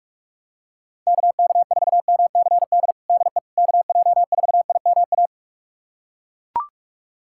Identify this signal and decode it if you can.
Morse code, heard as a keyed tone, about 42 wpm, 700 Hz. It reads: KK4MCD DE KJ4IMA
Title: USA Callsign DE Callsign - 6 Characters - 45wpm